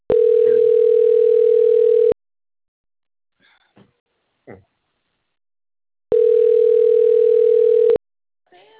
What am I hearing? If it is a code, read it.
Ringback tone (North America), 440 + 480 Hz.
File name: out- - - - - .70960